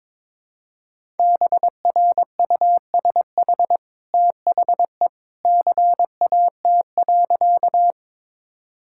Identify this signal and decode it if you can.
Morse code, heard as a keyed tone, about 21 wpm, 700 Hz. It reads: BRUSH THE CAT.
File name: practice / 22wpm / Easy Sentences with Hints - 22wpm